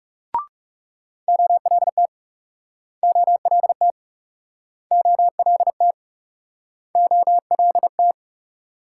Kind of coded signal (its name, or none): Morse code